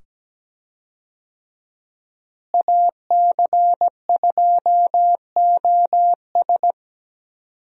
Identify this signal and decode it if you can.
Morse code, heard as a keyed tone, about 17 wpm, 700 Hz. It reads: AC2OS
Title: USA Call Signs - 5 Characters - Repeated 2x - 17wpm